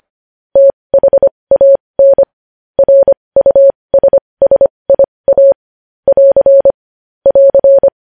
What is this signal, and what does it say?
Morse code, heard as a keyed tone, about 25 wpm, spaced as for 20 wpm, 550 Hz. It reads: THAN RUSSIA <AR> <AR>
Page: news-headlines-in-morse-code-at-20-wpm